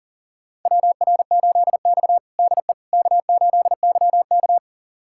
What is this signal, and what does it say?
Morse code, heard as a keyed tone, about 40 wpm, 690 Hz. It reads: WR8X DE K8YK
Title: USA Callsign DE Callsign - 4 Characters - 40wpm